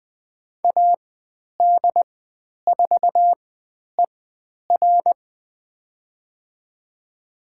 Morse code, heard as a keyed tone, about 20 wpm, spaced as for 10 wpm, 700 Hz. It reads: AD4ER
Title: 102 Lesson 17g - Letter u - Call Signs 20wpm